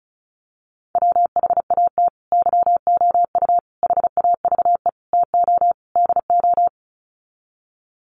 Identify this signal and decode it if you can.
Morse code, heard as a keyed tone, about 34 wpm, 700 Hz. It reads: WHAT YOU HAVE TO DO
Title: Sets of 5 Words Encore 35wpm